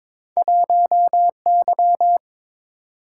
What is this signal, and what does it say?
Morse code, heard as a keyed tone, about 22 wpm, 700 Hz. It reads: 1Y
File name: practice / 22wpm / Two Letter-Number Combinations - 22wpm